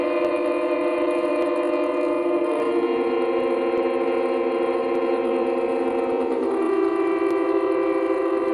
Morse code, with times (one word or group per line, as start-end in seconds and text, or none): none